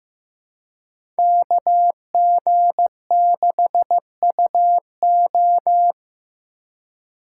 Morse code, heard as a keyed tone, about 15 wpm, 700 Hz. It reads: KG6UO
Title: USA Call Signs - 5 Characters - 2x Character Spacing - 15wpm